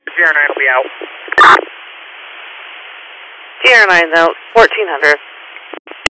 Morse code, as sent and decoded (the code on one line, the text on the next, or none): none